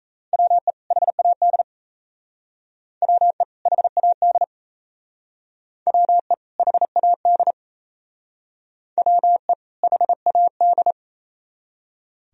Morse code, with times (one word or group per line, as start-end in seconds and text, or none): 0.3-0.7 WE
0.9-1.6 HAD
3.0-3.4 WE
3.7-4.4 HAD
5.9-6.3 WE
6.6-7.5 HAD
9.0-9.5 WE
9.8-10.9 HAD